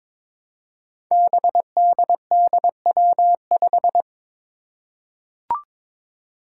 Morse code, heard as a keyed tone, about 23 wpm, 700 Hz. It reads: BDDW5
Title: Five Challenging Letter-Number Combinations 22wpm